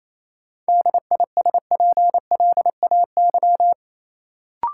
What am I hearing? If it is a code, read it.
Morse code, heard as a keyed tone, about 28 wpm, 700 Hz. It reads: DISPLAY
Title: Sentences from Top 1200 Words - New Word Drill - 28wpm